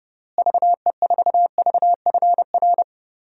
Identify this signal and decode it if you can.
Morse code, heard as a keyed tone, about 32 wpm, 700 Hz. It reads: VE4VFR
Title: DXCC-100 Call Signs - 6 Characters - Repeated 2x - 30wpm